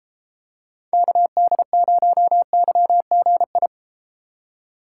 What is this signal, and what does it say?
Morse code, heard as a keyed tone, about 31 wpm, 700 Hz. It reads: KD0YGI